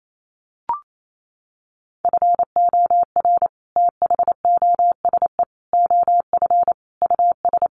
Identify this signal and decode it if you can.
Morse code, heard as a keyed tone, about 28 wpm, 700 Hz. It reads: FOR THOSE OF US
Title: Sets of 5 Words Encore 28wpm